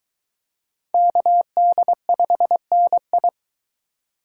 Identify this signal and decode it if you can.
Morse code, heard as a keyed tone, about 20 wpm, 700 Hz. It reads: KD5NI